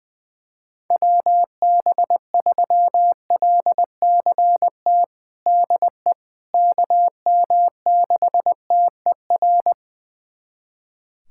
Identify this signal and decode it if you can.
Morse code, heard as a keyed tone, about 21 wpm, 700 Hz. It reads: WB3LCT DE KM6TER
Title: USA Callsign DE Callsign 20wpm